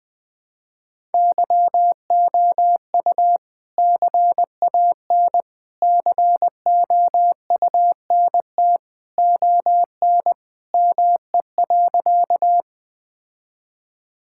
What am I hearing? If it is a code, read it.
Morse code, heard as a keyed tone, about 20 wpm, 700 Hz. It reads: YOU CAN COUNT ON ME.
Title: All-Age Appropriate Jokes - 20wpm